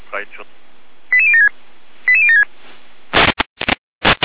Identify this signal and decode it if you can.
Five-tone selective call (ZVEI), heard as quick runs of five tones, tones of about 70 ms, with speech. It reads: 88076 88076